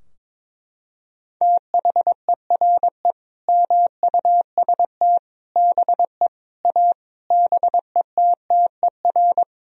Morse code, heard as a keyed tone, about 22 wpm, 700 Hz. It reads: THERE MUST BE A BETTER
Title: Sentences from Top 1000 Words - Review Part 4 - 22wpm